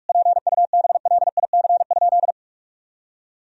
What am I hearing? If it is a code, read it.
Morse code, heard as a keyed tone, about 41 wpm, 700 Hz. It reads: PUBLIC?
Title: Sentences from Top 1000 Words - Review Part 3 - 45wpm